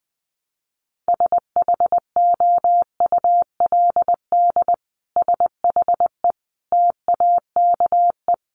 Morse code, heard as a keyed tone, about 20 wpm, 700 Hz. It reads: SHOULD SHE TAKE
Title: Sentences from Top 1200 Words 20wpm